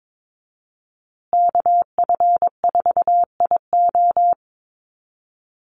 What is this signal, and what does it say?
Morse code, heard as a keyed tone, about 22 wpm, 700 Hz. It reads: KF4IO
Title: USA Call Signs Repeated 2x 22wpm